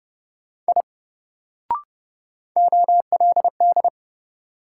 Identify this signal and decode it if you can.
Morse code, heard as a keyed tone, about 30 wpm, 700 Hz. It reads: I OLD
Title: Sentences from Top 200 Words - New Word Drill - 30wpm